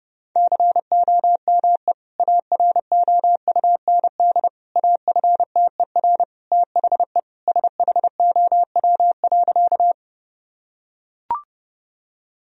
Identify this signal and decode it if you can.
Morse code, heard as a keyed tone, about 32 wpm, 700 Hz. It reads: COME AROUND AFTER THE SHOW.
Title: Sentences from Top 1000 Words - Review Part 3 - 30wpm